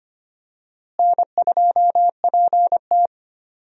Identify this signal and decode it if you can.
Morse code, heard as a keyed tone, about 25 wpm, 700 Hz. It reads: N2PT
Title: USA Call Signs - 4 Characters - Repeated 2x - 25wpm